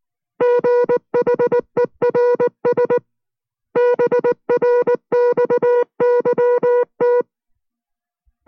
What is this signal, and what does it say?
Morse code, heard as a keyed tone, about 19 wpm, 490 Hz. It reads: GHERS BRXYT